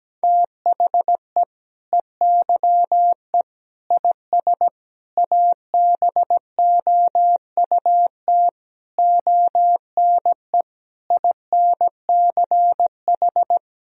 Morse code, heard as a keyed tone, about 17 wpm, 700 Hz. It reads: THE EYE IS ABOUT ONE INCH
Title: Sentences from Top 1000 Words - Review Part 3 - 17wpm